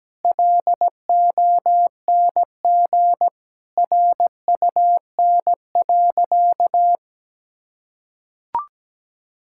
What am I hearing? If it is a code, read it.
Morse code, heard as a keyed tone, about 17 wpm, 700 Hz. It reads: LONG RUN.
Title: Sentences from Top 1000 Words - Review Part 6 - 17wpm